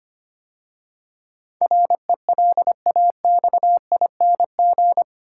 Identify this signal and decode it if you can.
Morse code, heard as a keyed tone, about 25 wpm, 700 Hz. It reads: RELAXING
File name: 8 Letter Words - First 3 Letters Given - 25wpm